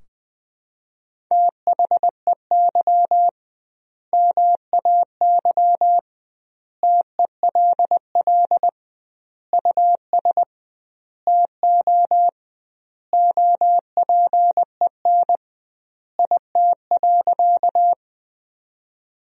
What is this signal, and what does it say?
Morse code, heard as a keyed tone, about 20 wpm, 700 Hz. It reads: THEY MAY TELL US TO OPEN IT.
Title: 2x Word-Spacing - Sentences from Top 300 Words - 20wpm